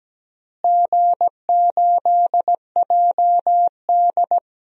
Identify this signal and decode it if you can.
Morse code, heard as a keyed tone, about 17 wpm, 700 Hz. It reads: G8JD
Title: DXCC-100 Call Signs - 4 Characters - Repeated 2x - 17wpm